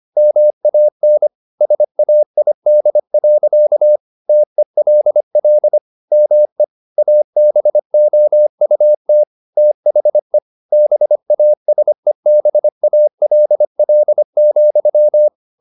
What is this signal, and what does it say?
Morse code, heard as a keyed tone, about 25 wpm, 600 Hz. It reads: MAN SAID. TELL ME ABOUT THE BASEBALL,
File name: the old man and the sea part 1 25wpm